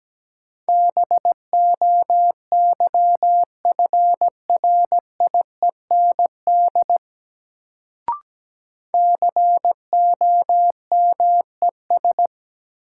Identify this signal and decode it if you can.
Morse code, heard as a keyed tone, about 17 wpm, 700 Hz. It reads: BOYFRIEND COMES
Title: Sentences from Top 1000 Words - Review Part 5 - New Word Drill - 17wpm